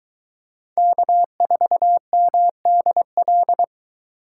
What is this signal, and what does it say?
Morse code, heard as a keyed tone, about 23 wpm, 700 Hz. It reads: K4MDL